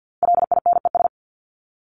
Morse code, heard as a keyed tone, about 50 wpm, 700 Hz. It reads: LINES
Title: ES:S Suffix - Top 100 Words with Base Words 50wpm